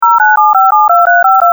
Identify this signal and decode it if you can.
DTMF, heard as telephone touch tones, tones of about 180 ms. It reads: 097572352
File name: voice / shenghuo